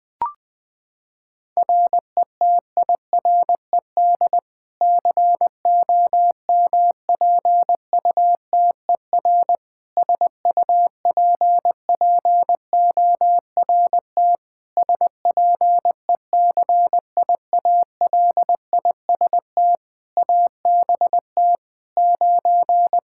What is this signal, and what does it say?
Morse code, heard as a keyed tone, about 20 wpm, 700 Hz. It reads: RETIRED COMPUTER SUPPORT SPECIALIST ABT 9
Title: Practice QSO - Exchange 3 - More Info - 20wpm